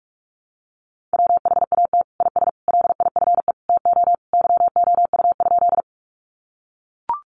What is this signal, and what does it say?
Morse code, heard as a keyed tone, about 44 wpm, 700 Hz. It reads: WHAT IS LIFE TO YOU?
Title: Sentences from Top 300 Words 45wpm